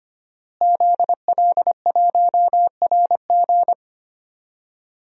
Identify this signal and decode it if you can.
Morse code, heard as a keyed tone, about 25 wpm, 700 Hz. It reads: ZL1RG
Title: DXCC-100 Call Signs - 5 Characters - Repeated 2x - 25wpm